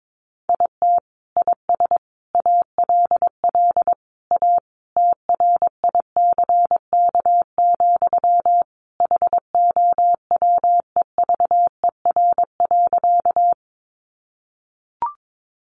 Morse code, heard as a keyed tone, about 22 wpm, 700 Hz. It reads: IT IS ALL A TRICK, HOWEVER.
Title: Sentences from Top 1500 Words 22wpm